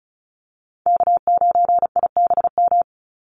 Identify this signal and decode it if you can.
Morse code, heard as a keyed tone, about 34 wpm, 700 Hz. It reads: K9IBM
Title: Call Signs Repeated 2x - 35wpm